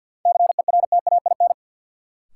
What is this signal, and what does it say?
Morse code, heard as a keyed tone, about 46 wpm, 700 Hz. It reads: CERTAIN
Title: Sentences from Top 500 Words - New Word Drill - 50wpm